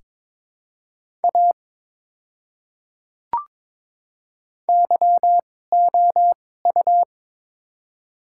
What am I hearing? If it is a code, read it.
Morse code, heard as a keyed tone, about 22 wpm, spaced as for 16 wpm, 700 Hz. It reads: A YOU
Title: 2x Character Spacing - Top 100 Words - 22wpm